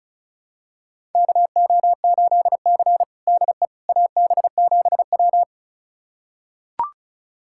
Morse code, heard as a keyed tone, about 36 wpm, 700 Hz. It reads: KO8C DE AB7W
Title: USA Callsign DE Callsign - 4 Characters - 35wpm